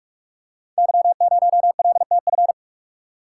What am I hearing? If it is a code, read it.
Morse code, heard as a keyed tone, about 44 wpm, 700 Hz. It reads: Y0LTF
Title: Code Groups - 5 Letter:Numbers - 45wpm